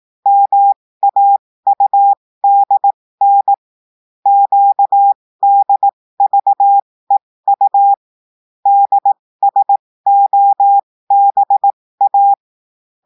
Morse code, heard as a keyed tone, about 18 wpm, spaced as for 15 wpm, 800 Hz. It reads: MAUDN QDVEU DSOBA
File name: cours-du-lundi files / cours-1580-texte-chiffre-15-mpm